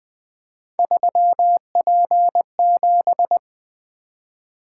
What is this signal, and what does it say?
Morse code, heard as a keyed tone, about 20 wpm, 700 Hz. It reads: 3P7
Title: Three Letter-Number Combinations - 20wpm